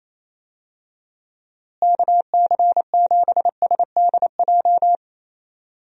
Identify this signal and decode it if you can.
Morse code, heard as a keyed tone, about 28 wpm, 700 Hz. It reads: KC7SDJ
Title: USA Call Signs - 6 Characters - Repeated 3x - 28wpm